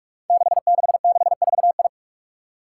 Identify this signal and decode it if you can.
Morse code, heard as a keyed tone, about 44 wpm, 700 Hz. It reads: 6664I
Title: Five Challenging Letter-Number Combinations 45wpm